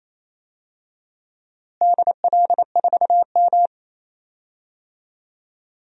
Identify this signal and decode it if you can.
Morse code, heard as a keyed tone, about 28 wpm, 700 Hz. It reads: DL4M
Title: DXCC-100 Call Signs - 4 Characters - Repeated 2x - 28wpm